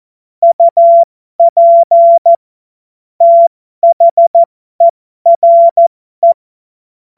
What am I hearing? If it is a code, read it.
Morse code, heard as a keyed tone, about 14 wpm, spaced as for 12 wpm, 680 Hz. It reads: UP THERE
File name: Morsekurs / Witze